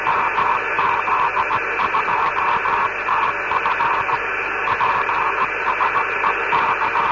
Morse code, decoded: none